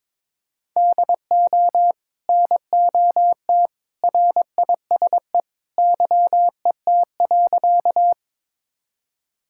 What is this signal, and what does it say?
Morse code, heard as a keyed tone, about 22 wpm, 700 Hz. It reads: DO NOT RISE YET.
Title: Sentences from Top 1000 Words - Review Part 6 - 22wpm